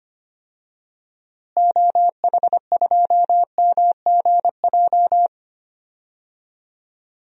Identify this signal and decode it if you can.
Morse code, heard as a keyed tone, about 25 wpm, 700 Hz. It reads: OH2MGJ
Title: DXCC-100 Call Signs - 6 Characters - Repeated 3x - 25wpm